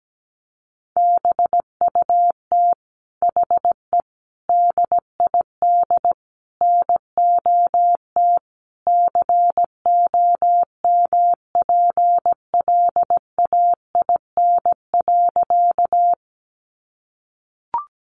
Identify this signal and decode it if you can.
Morse code, heard as a keyed tone, about 17 wpm, 700 Hz. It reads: BUT HE DID NOT COMPLAIN.
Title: Sentences from Top 1300 Words 17wpm